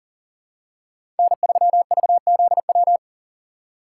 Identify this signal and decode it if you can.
Morse code, heard as a keyed tone, about 40 wpm, 700 Hz. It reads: N3VZW
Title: USA Call Signs - 5 Characters - 4x Character Spacing - 40wpm